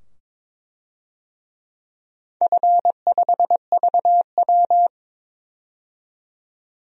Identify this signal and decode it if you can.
Morse code, heard as a keyed tone, about 22 wpm, 700 Hz. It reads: F5VW
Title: DXCC-100 Call Signs - 4 Characters - 22wpm